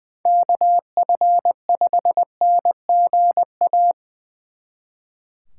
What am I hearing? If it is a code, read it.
Morse code, heard as a keyed tone, about 24 wpm, 700 Hz. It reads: KF5NGA
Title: Speed Racing - USA 2x3 Call Signs 20wpm